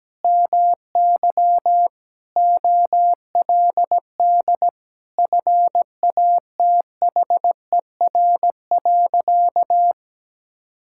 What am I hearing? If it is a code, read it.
Morse code, heard as a keyed tone, about 17 wpm, 700 Hz. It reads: MY OLD FATHER.
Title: Sentences from Top 1000 Words - Review Part 5 - 17wpm